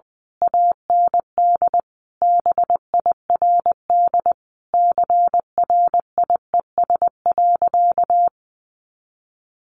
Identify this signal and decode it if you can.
Morse code, heard as a keyed tone, about 20 wpm, 700 Hz. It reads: AND BIRD CRIES.
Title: A Sound Of Thunder 20wpm